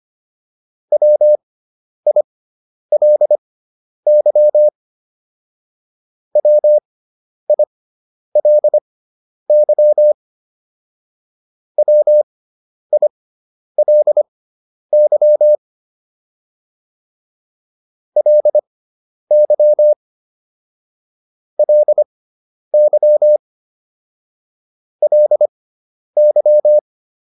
Morse code, heard as a keyed tone, about 25 wpm, spaced as for 10 wpm, 600 Hz. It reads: WILY WILY WILY LY LY LY